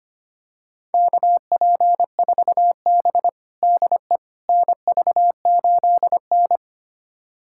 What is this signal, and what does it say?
Morse code, heard as a keyed tone, about 25 wpm, 700 Hz. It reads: KP4B DE NV8N
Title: USA Callsign DE Callsign - 4 Characters - 25wpm